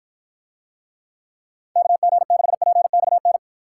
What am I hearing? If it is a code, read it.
Morse code, heard as a keyed tone, about 53 wpm, 700 Hz. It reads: KG6PXN